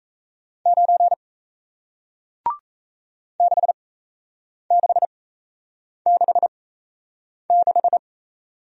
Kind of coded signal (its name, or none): Morse code